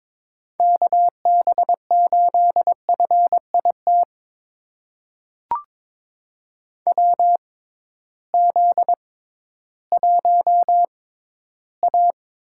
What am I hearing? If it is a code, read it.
Morse code, heard as a keyed tone, about 22 wpm, spaced as for 13 wpm, 700 Hz. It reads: KB8FIT WZ1A
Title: USA Call Signs - 6x Character Spacing - 22wpm